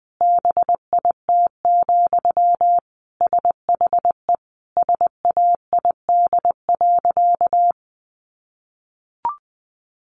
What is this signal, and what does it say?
Morse code, heard as a keyed tone, about 20 wpm, 700 Hz. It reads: BIT, SHE SAID.
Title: Sentences from Top 900 Words 20wpm